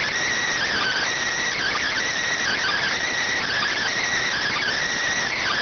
Emergency Alert System (SAME) tones, heard as harsh buzzing bursts. It is heard from 0.7 to 4.7 s.